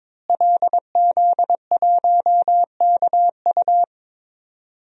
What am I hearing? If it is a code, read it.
Morse code, heard as a keyed tone, about 22 wpm, 700 Hz. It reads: LZ1KU
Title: DXCC-100 Call Signs - 5 Characters - Repeated 3x - 22wpm